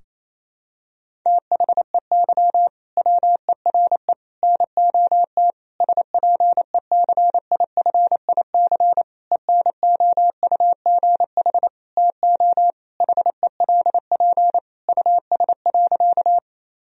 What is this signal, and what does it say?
Morse code, heard as a keyed tone, about 27 wpm, 700 Hz. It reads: THEY WERE NOT SPECIFIC ENOUGH TO HELP US.
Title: Sentences from Top 1000 Words - Review Part 3 - 28wpm